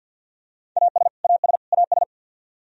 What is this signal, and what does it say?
Morse code, heard as a keyed tone, about 53 wpm, 700 Hz. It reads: AS AS AS
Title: Top 100 Words Repeated 50wpm